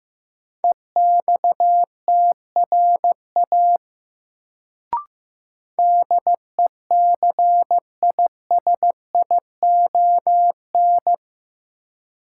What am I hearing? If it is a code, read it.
Morse code, heard as a keyed tone, about 15 wpm, 700 Hz. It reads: EXTRA DECISION
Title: Sentences from Top 1000 Words - New Word Drill - 15wpm